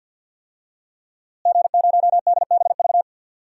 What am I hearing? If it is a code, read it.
Morse code, heard as a keyed tone, about 51 wpm, 700 Hz. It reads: G0DBV
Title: DXCC-100 Call Signs - 5 Characters - Repeated 2x - 50wpm